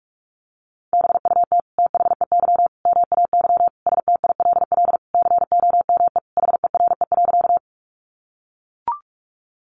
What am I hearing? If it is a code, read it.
Morse code, heard as a keyed tone, about 44 wpm, 700 Hz. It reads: BUT THEY MAY STILL COME HERE.
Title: Sentences from Top 300 Words 45wpm